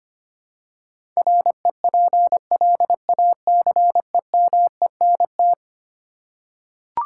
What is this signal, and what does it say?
Morse code, heard as a keyed tone, about 25 wpm, 700 Hz. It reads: REPLACEMENT
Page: morse-code-ninja-practice-25wpm